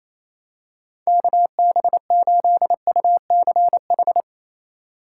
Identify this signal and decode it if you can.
Morse code, heard as a keyed tone, about 28 wpm, 700 Hz. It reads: KB8UCH
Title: USA Call Signs - 4x Character Spacing - 28wpm